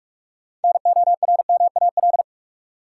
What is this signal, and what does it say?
Morse code, heard as a keyed tone, about 44 wpm, 700 Hz. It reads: NORMAL
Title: Sentences from Top 1200 Words - New Word Drill - 45wpm